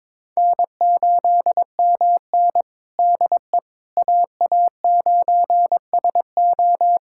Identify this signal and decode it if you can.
Morse code, heard as a keyed tone, about 22 wpm, 700 Hz. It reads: N8MN DE AA9SO
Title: USA Callsign DE Callsign 22wpm